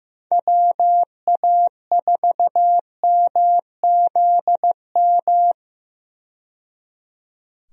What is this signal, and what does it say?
Morse code, heard as a keyed tone, about 15 wpm, 700 Hz. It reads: WA4MZM
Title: USA Call Signs - 6 Characters - 15wpm